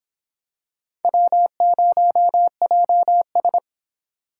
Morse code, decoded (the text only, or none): W0JS